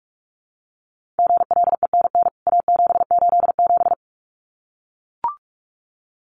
Morse code, heard as a keyed tone, about 44 wpm, 700 Hz. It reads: GLENN A787
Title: CWT Exchange 45wpm